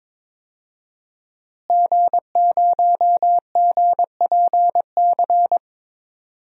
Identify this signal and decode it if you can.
Morse code, heard as a keyed tone, about 22 wpm, 700 Hz. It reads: G0GPC
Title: DXCC-100 Call Signs - 5 Characters - Repeated 3x - 22wpm